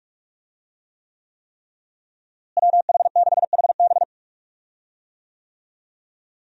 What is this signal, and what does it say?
Morse code, heard as a keyed tone, about 46 wpm, 700 Hz. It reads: WH6HB
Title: USA Call Signs - 5 Characters - Repeated 3x - 45wpm